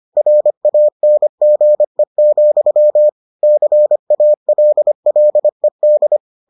Morse code, heard as a keyed tone, about 25 wpm, 600 Hz. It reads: RANGE, CALLED